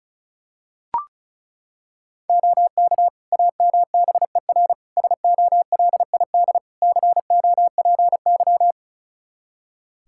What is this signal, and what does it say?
Morse code, heard as a keyed tone, about 36 wpm, 700 Hz. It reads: OK AMBER SOLID COPY
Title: Practice QSO - Exchange 3 - More Info - 35wpm